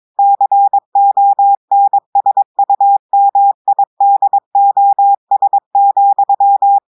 Morse code, heard as a keyed tone, about 22 wpm, 800 Hz. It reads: CONSUMIDOS,
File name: audios / portugais / portugais texte en clair 2 vitesse 22 mots minutes